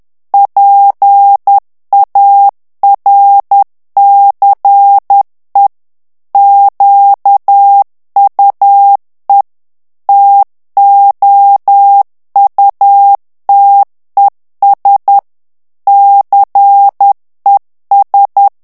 Morse code, heard as a keyed tone, about 10 wpm, 800 Hz. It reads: PARCE QUE TOUTES CES
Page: Français - texte en clair 1 - vitesse : 10 mots minute